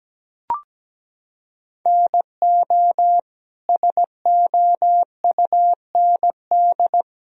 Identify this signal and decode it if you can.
Morse code, heard as a keyed tone, about 17 wpm, 700 Hz. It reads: NO SOUND
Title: Sentences from Top 1000 Words - Review Part 4 - 17wpm